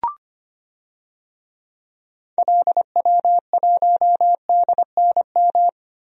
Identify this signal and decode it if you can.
Morse code, heard as a keyed tone, about 25 wpm, 700 Hz. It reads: LW1DNM
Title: DXCC-100 Call Signs - 6 Characters - 25wpm